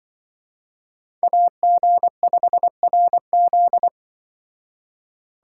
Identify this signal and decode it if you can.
Morse code, heard as a keyed tone, about 24 wpm, 700 Hz. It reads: AG5RZ
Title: Speed Racing - USA 2x2 Call Signs 20wpm